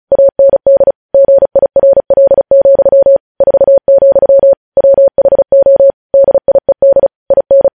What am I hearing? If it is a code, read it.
Morse code, heard as a keyed tone, about 36 wpm, 550 Hz. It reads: AND GIRL, 4, WHO DIED IN
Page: News headlines in Morse code at 35 wpm